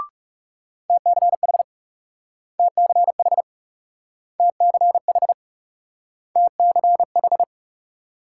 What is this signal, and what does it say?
Morse code, heard as a keyed tone, about 37 wpm, 700 Hz. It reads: TCH TCH TCH TCH
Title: Speed-Racing Multi-letter Phonemes - 30wpm